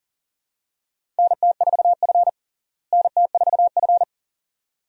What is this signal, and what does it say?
Morse code, heard as a keyed tone, about 40 wpm, 700 Hz. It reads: NT4F NT4F
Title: USA Call Signs - 4 Characters - Repeated 2x - 40wpm